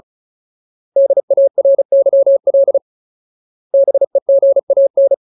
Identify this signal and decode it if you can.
Morse code, heard as a keyed tone, about 34 wpm, 550 Hz. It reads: DARYL BEGAN
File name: Ultimate Word List / 35WPM Ultimate